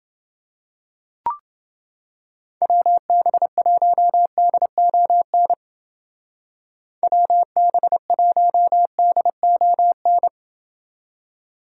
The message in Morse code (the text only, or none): WB1DON WB1DON